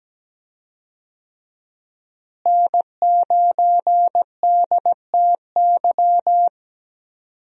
Morse code, heard as a keyed tone, about 17 wpm, 700 Hz. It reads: N9DTY
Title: USA Call Signs - 5 Characters - Repeated 3x - 17wpm